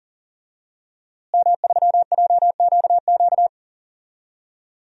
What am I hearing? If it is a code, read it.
Morse code, heard as a keyed tone, about 40 wpm, 700 Hz. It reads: M3JQQ
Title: DXCC-100 Call Signs - 5 Characters - Repeated 3x - 40wpm